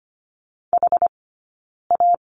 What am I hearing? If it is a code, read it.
Morse code, heard as a keyed tone, about 25 wpm, 700 Hz. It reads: H A
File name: Lesson 15d - Number 5 - Two Characters 25wpm